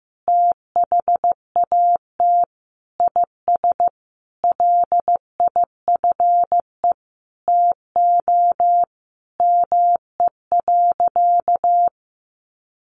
Morse code, heard as a keyed tone, about 15 wpm, 700 Hz. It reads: THAT IS LIFE TO ME.
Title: Sentences from Top 500 Words 15wpm